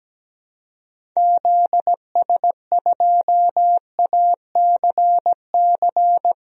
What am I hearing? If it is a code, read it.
Morse code, heard as a keyed tone, about 17 wpm, 700 Hz. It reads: ZS2ACC
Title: DXCC-100 Call Signs - 6 Characters - Repeated 3x - 17wpm